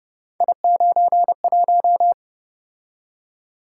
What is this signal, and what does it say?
Morse code, heard as a keyed tone, about 28 wpm, 700 Hz. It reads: I91